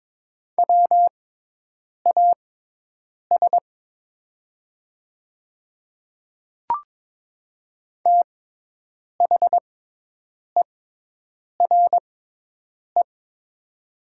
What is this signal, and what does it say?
Morse code, heard as a keyed tone, about 22 wpm, spaced as for 8 wpm, 700 Hz. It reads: WAS THERE
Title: 6x Character Spacing - Top 100 Words - 22wpm